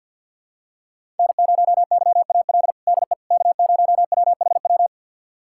Morse code, heard as a keyed tone, about 52 wpm, 700 Hz. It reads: N0YAL DE K0WHW
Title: USA Callsign DE Callsign - 5 Characters - 50wpm